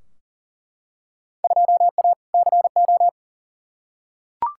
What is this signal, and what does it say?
Morse code, heard as a keyed tone, about 40 wpm, 700 Hz. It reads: 2A CO
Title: Field Day Exchanges - 40wpm